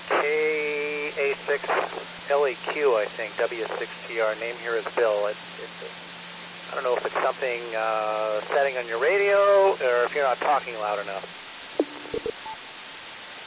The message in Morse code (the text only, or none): none